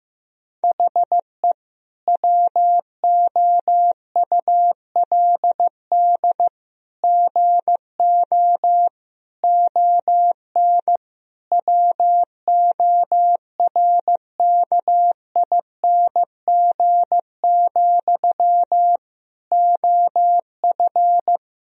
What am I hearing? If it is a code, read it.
Morse code, heard as a keyed tone, about 15 wpm, 700 Hz. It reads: HE WOULD GO ON WORKING, OF
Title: Sentences from Top 1000 Words - Review Part 4 - 15wpm